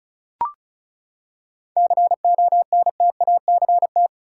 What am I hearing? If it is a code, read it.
Morse code, heard as a keyed tone, about 34 wpm, 700 Hz. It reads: CONTACT
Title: Seven Letter Words - 35wpm